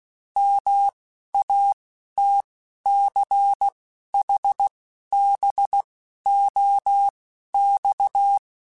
Morse code, heard as a keyed tone, about 16 wpm, spaced as for 11 wpm, 800 Hz. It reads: MATCHBOX
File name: QSOs / QSOs-mp3s / WD0FBT 1@10